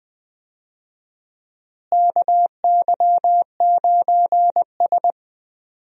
Morse code, heard as a keyed tone, about 21 wpm, 700 Hz. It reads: KY9S
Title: USA Call Signs - 4 Characters - Repeated 2x - 20wpm